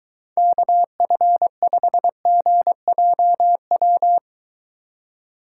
Morse code, heard as a keyed tone, about 23 wpm, 700 Hz. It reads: KF5GJW